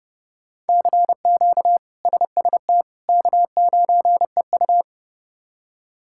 Morse code, heard as a keyed tone, about 32 wpm, 700 Hz. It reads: CQ SST K9EU
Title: K1USN SST QSO Practice - 30wpm